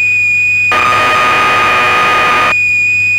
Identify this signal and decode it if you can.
Cassette data (Kansas City type).
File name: Acorn User/1984-12 Electron Programs